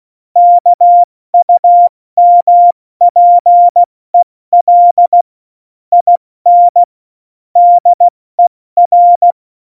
Morse code, heard as a keyed tone, about 16 wpm, 700 Hz. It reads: KUMPEL IN DER